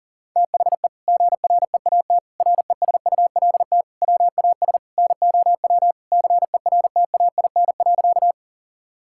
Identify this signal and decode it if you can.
Morse code, heard as a keyed tone, about 40 wpm, 700 Hz. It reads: THE GREAT RESULT WAS NOW CERTAIN.
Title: Sentences from Top 1000 Words - Review Part 5 - 40wpm